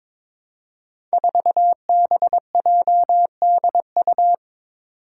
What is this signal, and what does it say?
Morse code, heard as a keyed tone, about 22 wpm, 700 Hz. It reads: 4BJDU
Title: Five Challenging Letter-Number Combinations 22wpm